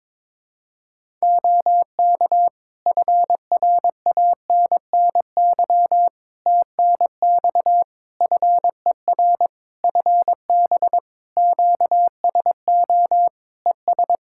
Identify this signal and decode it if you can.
Morse code, heard as a keyed tone, about 22 wpm, 700 Hz. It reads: OK FRANNY TNX FER FB QSO ES
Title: Practice QSO - Exchange 4 - Ending - 22wpm